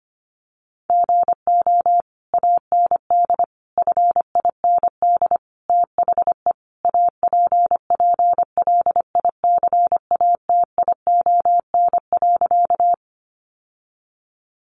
Morse code, heard as a keyed tone, about 25 wpm, 700 Hz. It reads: GO AND FIND THE APPLICATION.
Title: Sentences from Top 1200 Words 25wpm